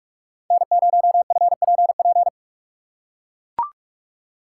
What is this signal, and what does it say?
Morse code, heard as a keyed tone, about 43 wpm, 700 Hz. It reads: N0FPP
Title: USA Call Signs - 3x Character Spacing - 45wpm